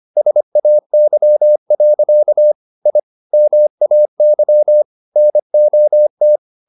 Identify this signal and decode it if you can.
Morse code, heard as a keyed tone, about 25 wpm, 600 Hz. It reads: SAY. I MAY NOT
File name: the old man and the sea part 1 25wpm